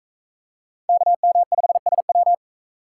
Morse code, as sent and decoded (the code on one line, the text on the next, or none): -.- -- ..... ... .--
KM5SW